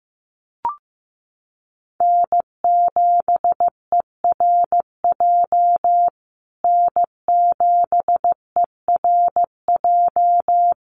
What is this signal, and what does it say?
Morse code, heard as a keyed tone, about 15 wpm, 700 Hz. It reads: N7ERJ N7ERJ
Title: USA Call Signs Repeated 2x 15wpm